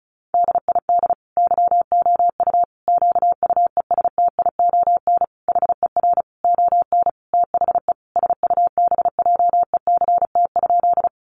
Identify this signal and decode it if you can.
Morse code, heard as a keyed tone, about 34 wpm, 700 Hz. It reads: DID YOU QUESTION HER ON THE SUBJECT?
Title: Sentences from Top 800 Words 35wpm